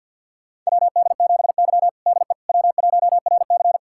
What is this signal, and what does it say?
Morse code, heard as a keyed tone, about 52 wpm, 700 Hz. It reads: WD7Y DE W1RC
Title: USA Callsign DE Callsign - 4 Characters - 50wpm